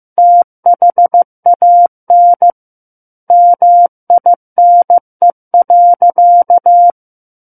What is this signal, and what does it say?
Morse code, heard as a keyed tone, about 15 wpm, 700 Hz. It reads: THAN MINE.
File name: Book One / 06-Chapter 6 - 15 WPM